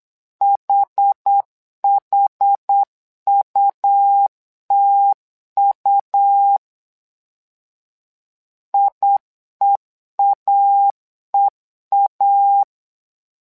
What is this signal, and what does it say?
Morse code, heard as a keyed tone, about 8 wpm, 800 Hz. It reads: HHUTU IEAEA
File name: LEZIONI / 2 UHDN / 2 UHD 8wpm 2258